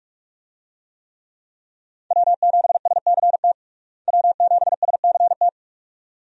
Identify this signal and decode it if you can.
Morse code, heard as a keyed tone, about 46 wpm, 700 Hz. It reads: W7SCT W7SCT
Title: USA Call Signs - 5 Characters - Repeated 2x - 45wpm